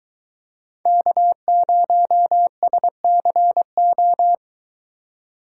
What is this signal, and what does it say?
Morse code, heard as a keyed tone, about 23 wpm, 700 Hz. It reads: K0SCO